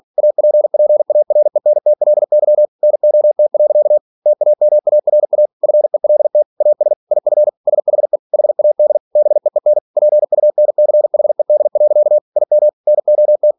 Morse code, heard as a keyed tone, about 46 wpm, 600 Hz. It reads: APPARENTLY NOT. TAMARA FELT AS IF SHE HAD BEEN PUNCHED. IM NOT